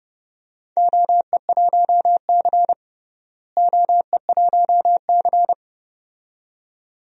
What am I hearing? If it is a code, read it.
Morse code, heard as a keyed tone, about 30 wpm, 700 Hz. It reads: OE1C OE1C
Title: DXCC-100 Call Signs - 4 Characters - Repeated 2x - 30wpm